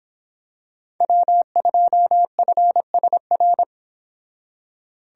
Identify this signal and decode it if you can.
Morse code, heard as a keyed tone, about 26 wpm, 700 Hz. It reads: W2FSR